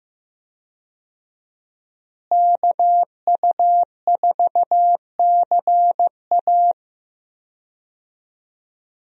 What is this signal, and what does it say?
Morse code, heard as a keyed tone, about 15 wpm, 700 Hz. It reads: KU4CA
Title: USA Call Signs - 5 Characters - Repeated 3x - 15wpm